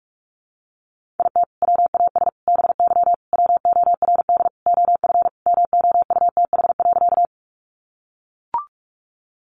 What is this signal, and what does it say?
Morse code, heard as a keyed tone, about 44 wpm, 700 Hz. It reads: IT WAS BY WORD OF MOUTH.
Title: Sentences from Top 1500 Words 45wpm